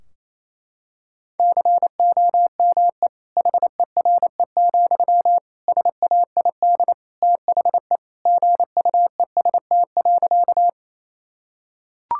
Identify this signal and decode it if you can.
Morse code, heard as a keyed tone, about 28 wpm, 700 Hz. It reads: COME HERE, SAID THE GUEST.
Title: Sentences from Top 1600 Words 28wpm